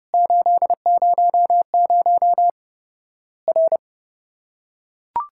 Morse code, heard as a keyed tone, about 30 wpm, 700 Hz. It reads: 800 R
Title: Worked All Europe DX Contest - QTC Exchange - 30wpm